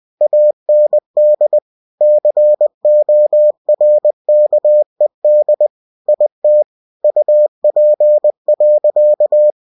Morse code, heard as a keyed tone, about 20 wpm, 600 Hz. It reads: AND CORKED IT UP.
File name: cw-books / winnie-the-pooh / winnie-20wpm-page-0095